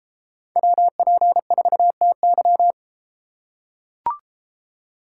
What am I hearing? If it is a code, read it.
Morse code, heard as a keyed tone, about 32 wpm, 700 Hz. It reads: WP4TY